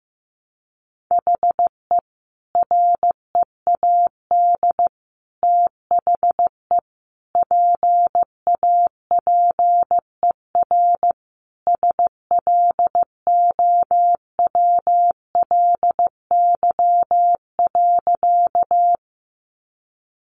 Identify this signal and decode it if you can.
Morse code, heard as a keyed tone, about 14 wpm, 700 Hz. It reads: HE READ THE PAPER SLOWLY.
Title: Sentences from Top 1100 Words 15wpm